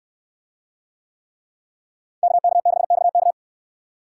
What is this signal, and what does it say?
Morse code, heard as a keyed tone, about 65 wpm, 700 Hz. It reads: KK6XK